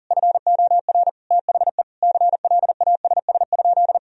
Morse code, heard as a keyed tone, about 40 wpm, 700 Hz. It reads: FOR THE CLASS?
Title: Sentences from Top 1000 Words - Review Part 2 - 40wpm